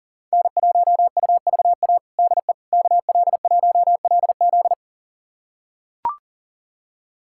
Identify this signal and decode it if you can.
Morse code, heard as a keyed tone, about 40 wpm, 700 Hz. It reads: N1UVA DE KL1LZ
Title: USA Callsign DE Callsign - 5 Characters - 40wpm